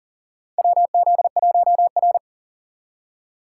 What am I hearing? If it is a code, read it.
Morse code, heard as a keyed tone, about 40 wpm, 700 Hz. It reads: WZ1R
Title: USA Call Signs - 4 Characters - 6x Character Spacing - 40wpm